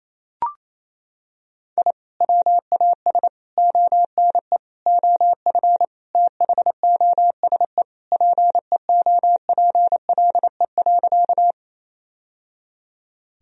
Morse code, heard as a keyed tone, about 28 wpm, 700 Hz. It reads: I WAS ONE OF THOSE PEOPLE.
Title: Sentences from Top 1000 Words - Review Part 1 - 28wpm